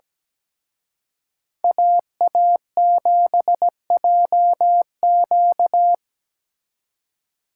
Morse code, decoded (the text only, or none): AA7JQ